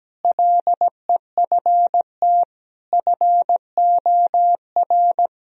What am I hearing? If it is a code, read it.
Morse code, heard as a keyed tone, about 17 wpm, 700 Hz. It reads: LEFT FOR
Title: Sentences from Top 1000 Words - Review Part 1 - 17wpm